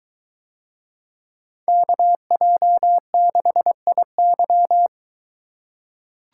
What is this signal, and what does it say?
Morse code, heard as a keyed tone, about 22 wpm, 700 Hz. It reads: KJ6IY